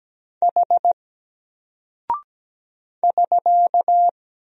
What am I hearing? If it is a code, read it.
Morse code, heard as a keyed tone, about 17 wpm, 700 Hz. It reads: H <SK>
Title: Single Character (Letter:Number:Punctuation:Prosign) Alt 17wpm